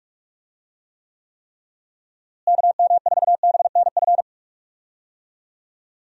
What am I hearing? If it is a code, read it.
Morse code, heard as a keyed tone, about 45 wpm, 700 Hz. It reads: KM4BNF
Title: USA Call Signs - 6 Characters - Repeated 3x - 45wpm